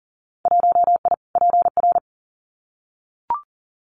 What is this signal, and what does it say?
Morse code, heard as a keyed tone, about 40 wpm, 700 Hz. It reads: 1I PR
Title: Winter Field Day Exchanges Encore 40wpm